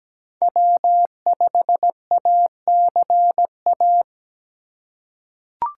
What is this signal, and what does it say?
Morse code, heard as a keyed tone, about 17 wpm, 700 Hz. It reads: W5ACA
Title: USA Call Signs - 5 Characters - 6x Character Spacing - 17wpm